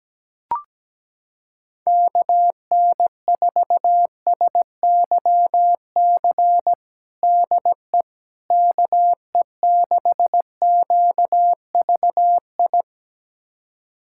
Morse code, heard as a keyed tone, about 17 wpm, 700 Hz. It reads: KN4SYC DE KE6QVI
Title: USA Callsign DE Callsign - 6 Characters - 17wpm